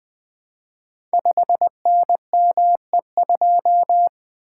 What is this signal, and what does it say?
Morse code, heard as a keyed tone, about 20 wpm, 700 Hz. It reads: 5NME2
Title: Code Groups - 5 Letter:Numbers - 20wpm